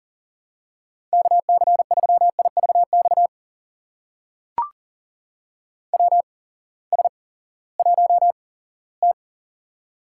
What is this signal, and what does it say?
Morse code, heard as a keyed tone, about 40 wpm, spaced as for 20 wpm, 700 Hz. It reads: KC3IVX WS1T
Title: USA Call Signs - 8x Character Spacing - 40wpm